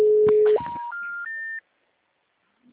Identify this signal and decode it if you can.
Special information tone.